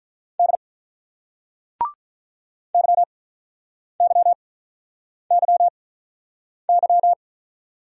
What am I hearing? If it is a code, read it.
Morse code, heard as a keyed tone, about 46 wpm, spaced as for 9 wpm, 700 Hz. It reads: D YYYY